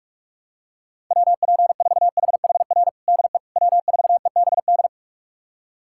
Morse code, heard as a keyed tone, about 46 wpm, 700 Hz. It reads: WP4HHR DE W4EBD
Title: USA Callsign DE Callsign 45wpm